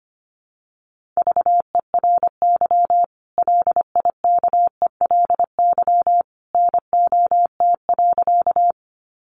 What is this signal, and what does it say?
Morse code, heard as a keyed tone, about 25 wpm, 700 Hz. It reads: VERY LIKELY NOT.
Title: Sentences from Top 600 Words 25wpm